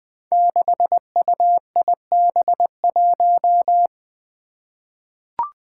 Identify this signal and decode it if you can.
Morse code, heard as a keyed tone, about 20 wpm, 700 Hz. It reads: 6UIB1
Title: Five Challenging Letter-Number Combinations 20wpm